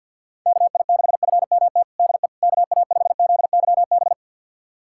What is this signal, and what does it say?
Morse code, heard as a keyed tone, about 46 wpm, 700 Hz. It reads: KI6FMT DE KA5ZYB
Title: USA Callsign DE Callsign - 6 Characters - 50wpm